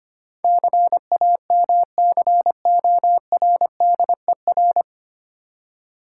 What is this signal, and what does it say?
Morse code, heard as a keyed tone, about 25 wpm, 700 Hz. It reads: CAMCORDER
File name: practice / 25wpm / Difficult Words - Speed-Racing with Hints - 25wpm